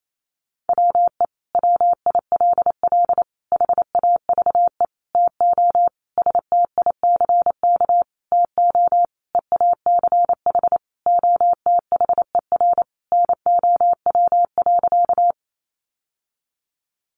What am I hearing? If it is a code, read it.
Morse code, heard as a keyed tone, about 28 wpm, 700 Hz. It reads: WE WILL HAVE TO STICK TO EACH OTHER NOW.
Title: Sentences from Top 1200 Words 28wpm